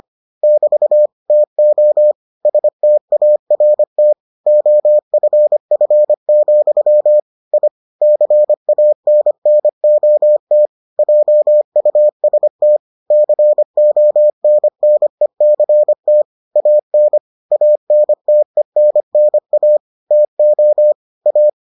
Morse code, heard as a keyed tone, about 25 wpm, 600 Hz. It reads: <BT> TO START OFF, I CANNOT JUST CONNECT AN ANTENNA TO A